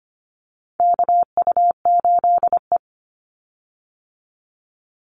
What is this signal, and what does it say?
Morse code, heard as a keyed tone, about 25 wpm, 700 Hz. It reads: KU8E
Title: Call Signs Repeated 3x - 25wpm